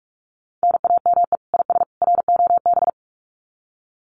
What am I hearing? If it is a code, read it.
Morse code, heard as a keyed tone, about 44 wpm, 700 Hz. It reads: NAME IS ROB
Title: Common QSO Phrases 45wpm